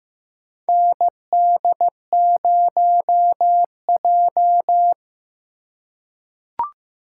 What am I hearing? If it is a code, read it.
Morse code, heard as a keyed tone, about 15 wpm, 700 Hz. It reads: ND0J
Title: USA Call Signs - 4 Characters - 2x Character Spacing - 15wpm